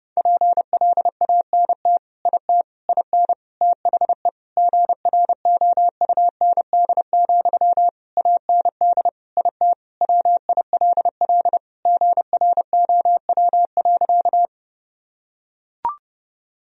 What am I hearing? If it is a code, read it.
Morse code, heard as a keyed tone, about 32 wpm, 700 Hz. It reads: PLANT IT IN THE GROUND, AND IT WILL GROW.
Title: Sentences from Top 1000 Words - Review Part 2 - 30wpm